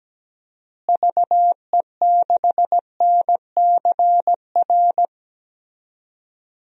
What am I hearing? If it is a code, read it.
Morse code, heard as a keyed tone, about 17 wpm, 700 Hz. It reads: VE6NCR
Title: DXCC-100 Call Signs - 6 Characters - Repeated 3x - 17wpm